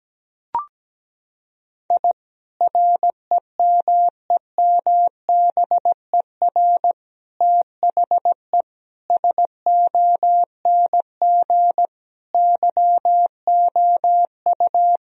Morse code, heard as a keyed tone, about 17 wpm, 700 Hz. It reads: I REMEMBER THE SONG YOU
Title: Sentences from Top 1000 Words - Review Part 2 - 17wpm